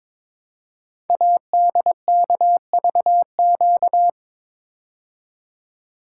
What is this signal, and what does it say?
Morse code, heard as a keyed tone, about 22 wpm, 700 Hz. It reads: ADKVQ
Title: Code Groups - 5 Letters - 22wpm